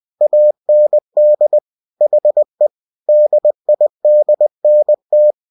Morse code, heard as a keyed tone, about 20 wpm, 600 Hz. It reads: AND HE DIDNT